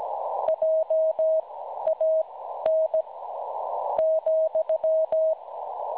Morse code, heard as a keyed tone, about 17 wpm, spaced as for 12 wpm, 640 Hz. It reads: JAN ,
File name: cwronde / websdr recording .6kHz